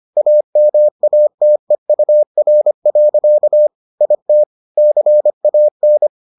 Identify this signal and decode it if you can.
Morse code, heard as a keyed tone, about 25 wpm, 600 Hz. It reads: AMATEUR. IT CAN